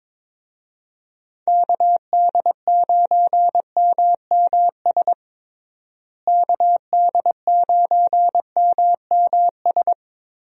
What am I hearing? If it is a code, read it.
Morse code, heard as a keyed tone, about 22 wpm, 700 Hz. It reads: KD9MMS KD9MMS
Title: USA Call Signs - 6 Characters - Repeated 3x - 22wpm